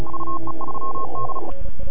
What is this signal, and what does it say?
Morse code, heard as a keyed tone, about 36 wpm, 1070 Hz, with noise. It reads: UE3D
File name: NAQPCW-8-2009 / W1UJ NAQP CW 09 0100Z